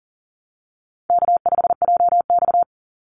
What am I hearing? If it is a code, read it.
Morse code, heard as a keyed tone, about 40 wpm, 700 Hz. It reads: K5JX
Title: Easy Call Signs - 40wpm